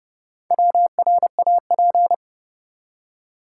Morse code, heard as a keyed tone, about 32 wpm, 700 Hz. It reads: WRAP
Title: Four Letter Words - 30wpm